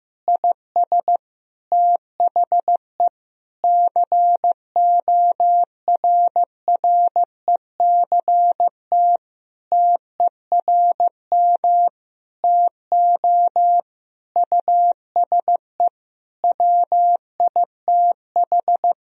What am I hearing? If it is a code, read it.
Morse code, heard as a keyed tone, about 15 wpm, 700 Hz. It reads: IS THE CORRECT TERM TO USE WITH
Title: Sentences from Top 1000 Words - Review Part 4 - 15wpm